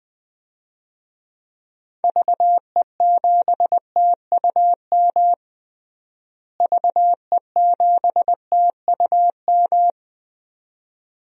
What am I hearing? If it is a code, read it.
Morse code, heard as a keyed tone, about 20 wpm, 700 Hz. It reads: VE7TUM VE7TUM
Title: DXCC-100 Call Signs - 6 Characters - Repeated 2x - 20wpm